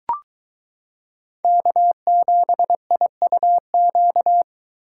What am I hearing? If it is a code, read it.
Morse code, heard as a keyed tone, about 23 wpm, 700 Hz. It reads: K7IUQ